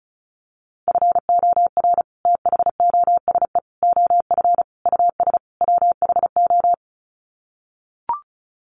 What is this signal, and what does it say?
Morse code, heard as a keyed tone, about 34 wpm, 700 Hz. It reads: FOR THOSE OF US WHO
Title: Sets of 5 Words Encore 35wpm